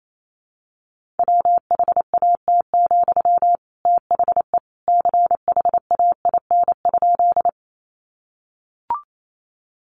Morse code, heard as a keyed tone, about 28 wpm, 700 Hz. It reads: WHAT, THE CHAIN?
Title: Sentences from Top 1500 Words 28wpm